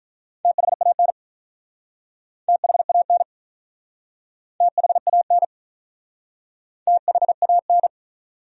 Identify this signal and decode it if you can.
Morse code, heard as a keyed tone, about 43 wpm, 700 Hz. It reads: THAN THAN THAN THAN
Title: Speed-Racing Top 200 Words 35wpm